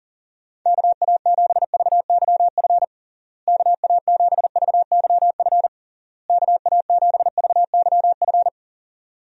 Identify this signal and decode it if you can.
Morse code, heard as a keyed tone, about 37 wpm, 700 Hz. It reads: KA7VYF KA7VYF KA7VYF
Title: USA Call Signs - 6 Characters - Repeated 3x - 40wpm